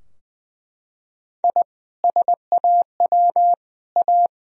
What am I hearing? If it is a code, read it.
Morse code, heard as a keyed tone, about 20 wpm, 700 Hz. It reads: I SAW A
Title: Sentences from Top 1000 Words - Review Part 2 - 20wpm